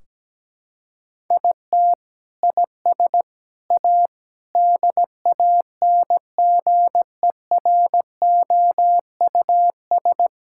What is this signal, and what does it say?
Morse code, heard as a keyed tone, about 17 wpm, 700 Hz. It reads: IT IS A DANGEROUS
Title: Sentences from Top 1600 Words 17wpm